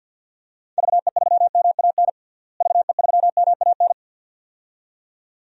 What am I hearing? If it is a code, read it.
Morse code, heard as a keyed tone, about 50 wpm, 700 Hz. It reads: VE3MAN VE3MAN
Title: DXCC-100 Call Signs - 6 Characters - Repeated 2x - 50wpm